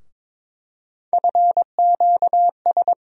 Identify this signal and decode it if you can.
Morse code, heard as a keyed tone, about 23 wpm, 700 Hz. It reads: FQS
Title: USA Call Sign Postfixes - 3 Letters 22wpm